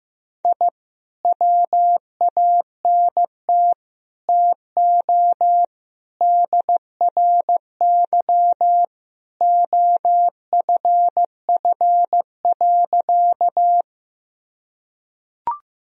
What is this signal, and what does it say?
Morse code, heard as a keyed tone, about 15 wpm, 700 Hz. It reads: I WANT TO DRY OFF.
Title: Sentences from Top 1000 Words - Review Part 3 - 15wpm